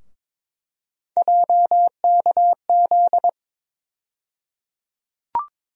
Morse code, heard as a keyed tone, about 22 wpm, 700 Hz. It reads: JKZ
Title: USA Call Sign Postfixes - 3 Letters 22wpm